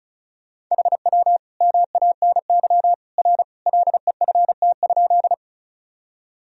Morse code, heard as a keyed tone, about 36 wpm, 700 Hz. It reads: HW MANY R LEFT?
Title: Sentences from Top 300 Words and Abbreviations 35wpm